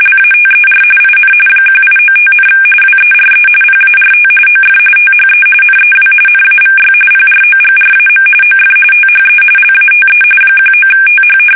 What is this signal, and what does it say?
RTTY, about 45.45 baud. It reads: LATER IN 1895, WERE PART OF THE NEW NAVY PROGRAM OF THE LATE 19TH
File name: fsk850 / battleships